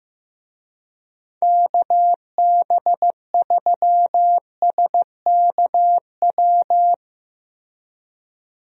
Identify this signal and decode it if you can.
Morse code, heard as a keyed tone, about 15 wpm, 700 Hz. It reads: KB3SKW
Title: USA Call Signs - 6 Characters - Repeated 3x - 15wpm